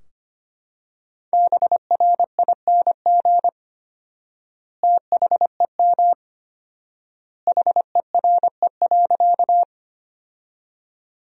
Morse code, heard as a keyed tone, about 25 wpm, 700 Hz. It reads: BRING THEM HERE.
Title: 4x Word-Spacing - Sentences from Top 500 Words - 25wpm